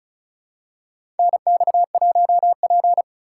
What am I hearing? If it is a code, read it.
Morse code, heard as a keyed tone, about 36 wpm, 700 Hz. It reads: NX1P
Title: USA Call Signs - 4 Characters - 3x Character Spacing - 35wpm